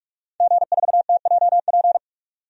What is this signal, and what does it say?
Morse code, heard as a keyed tone, about 42 wpm, 700 Hz. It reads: G4TJP
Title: DXCC-100 Call Signs - 5 Characters - 45wpm